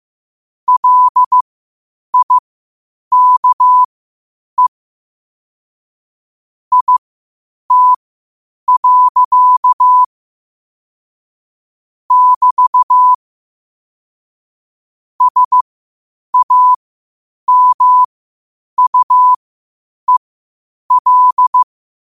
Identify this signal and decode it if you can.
Morse code, heard as a keyed tone, about 15 wpm, spaced as for 8 wpm, 1000 Hz. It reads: LIKE IT. <BT> SAMUEL